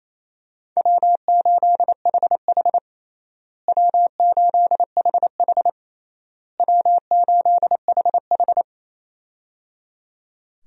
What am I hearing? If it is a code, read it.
Morse code, heard as a keyed tone, about 28 wpm, 700 Hz. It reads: W8HH W8HH W8HH
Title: USA Call Signs - 4 Characters - Repeated 3x - 28wpm